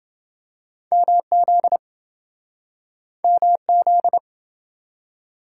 Morse code, heard as a keyed tone, about 28 wpm, 700 Hz. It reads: MZ MZ